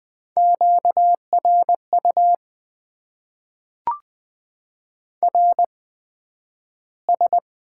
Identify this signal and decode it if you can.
Morse code, heard as a keyed tone, about 20 wpm, spaced as for 11 wpm, 700 Hz. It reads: QRU RS
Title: Top 90 QSO Elements - 8x Character Spacing - 20wpm